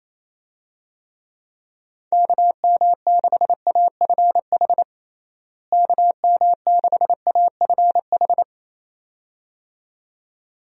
Morse code, heard as a keyed tone, about 28 wpm, 700 Hz. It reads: KM6AFH KM6AFH
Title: USA Call Signs - 6 Characters - Repeated 2x - 28wpm